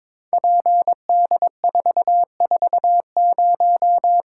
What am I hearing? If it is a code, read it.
Morse code, heard as a keyed tone, about 22 wpm, 700 Hz. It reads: PD440
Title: Code Groups - 5 Letter:Numbers - 22wpm